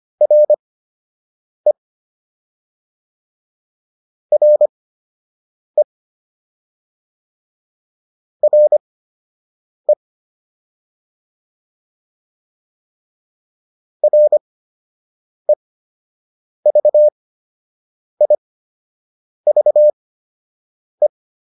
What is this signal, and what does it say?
Morse code, heard as a keyed tone, about 25 wpm, spaced as for 7 wpm, 600 Hz. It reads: RE RE RE REVIVE